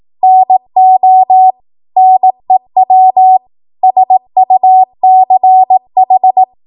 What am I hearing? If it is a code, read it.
Morse code, heard as a keyed tone, about 18 wpm, 750 Hz. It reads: NO NEW SUCH
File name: Morse / 18WPM